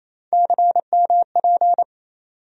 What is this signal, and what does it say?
Morse code, heard as a keyed tone, about 27 wpm, 700 Hz. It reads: CMP
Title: USA Call Sign Postfixes - 3 Letters 28wpm